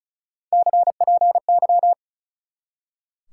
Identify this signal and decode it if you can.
Morse code, heard as a keyed tone, about 43 wpm, 700 Hz. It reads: CPY
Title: Speed-Racing Top 180 QSO Elements 35wpm